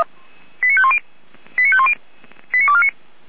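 Five-tone selective call (ZVEI), heard as quick runs of five tones, tones of about 70 ms. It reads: 89510 89510 89249
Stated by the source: Five-tone ZVEI selective call 89276 received by SDR, server of Feuerwehr Orlamünde.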